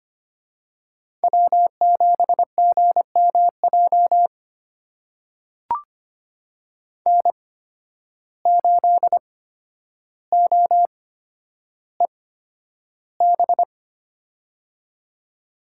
Morse code, heard as a keyed tone, about 25 wpm, spaced as for 11 wpm, 700 Hz. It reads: W7GMJ N8OEB
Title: USA Call Signs - 5 Characters - 8x Character Spacing - 25wpm